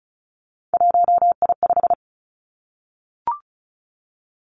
Morse code, heard as a keyed tone, about 36 wpm, 700 Hz. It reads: 1I5
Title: Three Challenging Letter-Number Combinations 35wpm